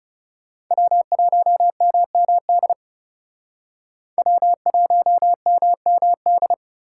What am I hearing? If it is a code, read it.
Morse code, heard as a keyed tone, about 36 wpm, 700 Hz. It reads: W1MMD W1MMD
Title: Speed Racing - USA 1x3 Call Signs 30wpm